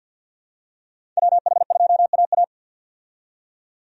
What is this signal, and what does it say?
Morse code, heard as a keyed tone, about 48 wpm, 700 Hz. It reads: WH2AA
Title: USA Call Signs - 5 Characters - 50wpm